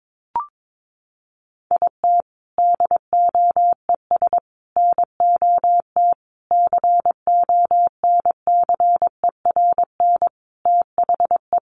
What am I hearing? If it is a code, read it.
Morse code, heard as a keyed tone, about 22 wpm, 700 Hz. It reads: IT DOES NOT CONCERN THE
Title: Sentences from Top 1500 Words 22wpm